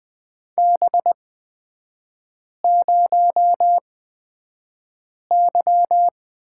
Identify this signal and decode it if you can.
Morse code, heard as a keyed tone, about 20 wpm, spaced as for 5 wpm, 700 Hz. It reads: B0Y
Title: 236 Lesson 38e - Letter q - Three Characters 20wpm